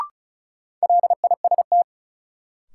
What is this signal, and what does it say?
Morse code, heard as a keyed tone, about 36 wpm, 700 Hz. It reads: LIST
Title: Sentences from Top 400 Words - New Word Drill - 35wpm